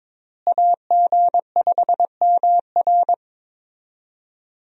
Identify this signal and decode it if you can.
Morse code, heard as a keyed tone, about 22 wpm, 700 Hz. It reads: AG5MR
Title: USA Call Signs - 5 Characters - Repeated 3x - 22wpm